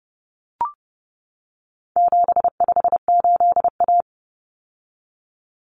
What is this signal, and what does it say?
Morse code, heard as a keyed tone, about 30 wpm, 700 Hz. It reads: 758A
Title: Common Cut Numbers - 4-Digit 30wpm